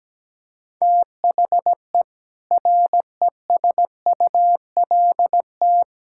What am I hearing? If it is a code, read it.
Morse code, heard as a keyed tone, about 17 wpm, 700 Hz. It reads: THE RESULT
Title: Sentences from Top 1000 Words - Review Part 5 - 17wpm